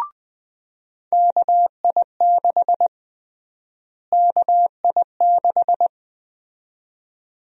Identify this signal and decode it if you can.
Morse code, heard as a keyed tone, about 21 wpm, 700 Hz. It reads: KI6 KI6
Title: USA Call Sign Prefixes - Repeated 2x - 20wpm